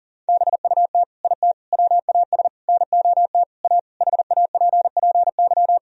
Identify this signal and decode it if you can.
Morse code, heard as a keyed tone, about 40 wpm, 700 Hz. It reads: BUT IT WAS NOT A HAPPY
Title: Sentences from Top 1000 Words - Review Part 1 - 40wpm